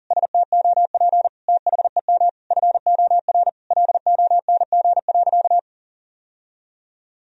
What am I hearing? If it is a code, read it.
Morse code, heard as a keyed tone, about 40 wpm, 700 Hz. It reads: STOP THEM FOR LONG.
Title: Sentences from Top 1000 Words - Review Part 1 - 40wpm